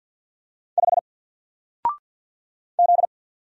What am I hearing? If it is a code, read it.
Morse code, heard as a keyed tone, about 50 wpm, 700 Hz. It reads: 5 Z
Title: Single Letter-Number 50wpm